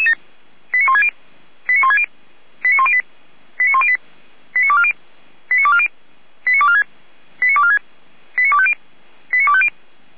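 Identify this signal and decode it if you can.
Five-tone selective call (ZVEI), heard as quick runs of five tones, tones of about 70 ms. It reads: 89160 89160 89108 89108 89240 89240 89246 89246 89250 89250
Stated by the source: ZVEI five-tone selective call received on an SDR.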